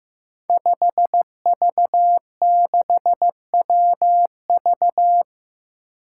Morse code, heard as a keyed tone, about 15 wpm, 700 Hz. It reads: 5V6WV
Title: Five Challenging Letter-Number Combinations 15wpm